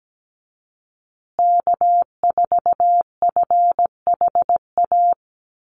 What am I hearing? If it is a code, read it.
Morse code, heard as a keyed tone, about 17 wpm, 700 Hz. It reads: K4FHA
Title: USA Call Signs Repeated 2x 17wpm